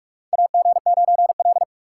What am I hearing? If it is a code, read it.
Morse code, heard as a keyed tone, about 46 wpm, 700 Hz. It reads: AG9L
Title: USA Call Signs - 4 Characters - 4x Character Spacing - 45wpm